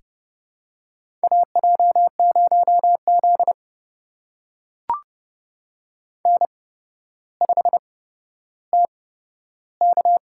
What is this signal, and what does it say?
Morse code, heard as a keyed tone, about 32 wpm, spaced as for 13 wpm, 700 Hz. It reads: AJ0Z N5TK
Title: USA Call Signs - 4 Characters - 8x Character Spacing - 30wpm